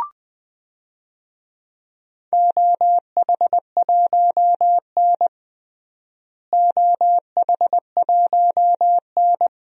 Morse code, heard as a keyed tone, about 20 wpm, 700 Hz. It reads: OH1N OH1N
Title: DXCC-100 Call Signs - 4 Characters - Repeated 3x - 20wpm